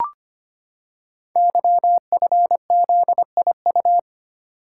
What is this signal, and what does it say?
Morse code, heard as a keyed tone, about 25 wpm, 700 Hz. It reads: YFZIU
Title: Code Groups - 5 Letters - 25wpm